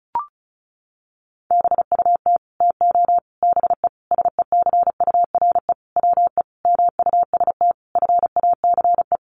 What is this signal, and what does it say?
Morse code, heard as a keyed tone, about 34 wpm, 700 Hz. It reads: BUT TO BE SECURE WE MUST FACE
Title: Sentences from Top 1200 Words 35wpm